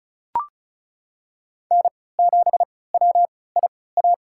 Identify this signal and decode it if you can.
Morse code, heard as a keyed tone, about 36 wpm, spaced as for 20 wpm, 690 Hz. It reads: N7WIA
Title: USA Call Signs - 3x Character Spacing - 35wpm